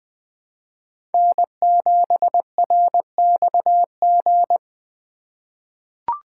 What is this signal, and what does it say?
Morse code, heard as a keyed tone, about 20 wpm, 700 Hz. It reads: N7RXG
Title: USA Call Signs - 5 Characters - 2x Character Spacing - 20wpm